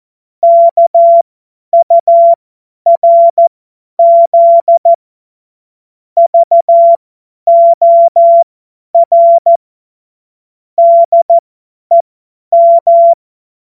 Morse code, heard as a keyed tone, about 14 wpm, spaced as for 10 wpm, 680 Hz. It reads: KURZ VOR DEM
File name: Morsekurs / Witze / Witz 17 Juni 2025 50 F70